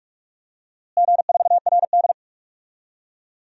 Morse code, heard as a keyed tone, about 46 wpm, 700 Hz. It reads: G4RD
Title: DXCC-100 Call Signs - 4 Characters - Repeated 3x - 45wpm